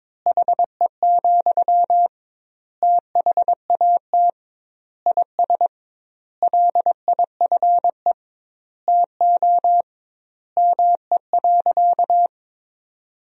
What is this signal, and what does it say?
Morse code, heard as a keyed tone, about 21 wpm, 700 Hz. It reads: HE, THAT IS LIFE TO ME.
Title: 2x Word-Spacing - Sentences from Top 500 Words - 22wpm